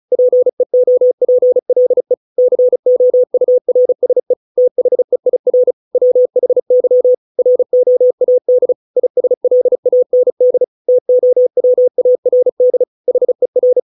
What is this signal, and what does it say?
Morse code, heard as a keyed tone, about 34 wpm, 500 Hz. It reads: PEOPLE COURSE THEIR WHY ROAD ISLAND TOWARD HER